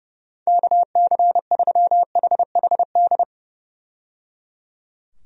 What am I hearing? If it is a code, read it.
Morse code, heard as a keyed tone, about 30 wpm, 700 Hz. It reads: KC3HHD
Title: USA Call Signs - 6 Characters - 30wpm